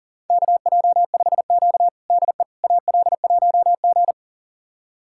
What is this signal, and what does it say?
Morse code, heard as a keyed tone, about 37 wpm, 700 Hz. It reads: KJ5Q DE AL1G
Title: USA Callsign DE Callsign - 4 Characters - 40wpm